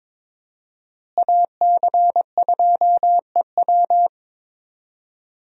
Morse code, heard as a keyed tone, about 22 wpm, 700 Hz. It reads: AC2EW
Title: USA Call Signs - 5 Characters - Repeated 3x - 22wpm